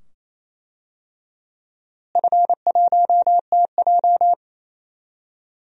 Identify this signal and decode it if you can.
Morse code, heard as a keyed tone, about 28 wpm, 700 Hz. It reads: F1TJ
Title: DXCC-100 Call Signs - 4 Characters - Repeated 2x - 28wpm